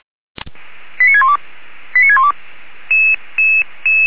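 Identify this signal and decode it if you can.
Five-tone selective call (ZVEI), heard as quick runs of five tones, tones of about 70 ms. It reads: 89612 89612
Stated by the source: ZVEI five-tone selective call received by SDR, address 89131.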